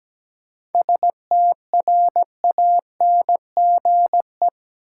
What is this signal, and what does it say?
Morse code, heard as a keyed tone, about 17 wpm, 700 Hz. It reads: STRANGE
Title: Seven Letter Words - 17wpm